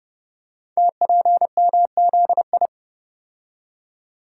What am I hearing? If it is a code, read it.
Morse code, heard as a keyed tone, about 30 wpm, 700 Hz. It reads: TPMZI
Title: Code Groups - 5 Letters - 30wpm